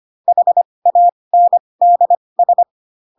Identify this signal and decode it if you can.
Morse code, heard as a keyed tone, about 25 wpm, spaced as for 20 wpm, 700 Hz. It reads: HANDS
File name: Pinocchio / Pinocchio ch2 15 25 ik1hge